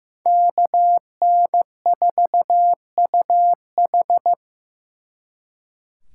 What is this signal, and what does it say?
Morse code, heard as a keyed tone, about 18 wpm, 700 Hz. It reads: KN4UH
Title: Speed Racing - USA Call Signs 15wpm